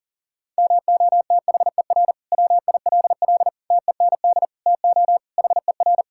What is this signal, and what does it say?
Morse code, heard as a keyed tone, about 37 wpm, 700 Hz. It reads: MOTHER WILL TEND TO HER
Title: Sentences from Top 1000 Words - Review Part 5 - 40wpm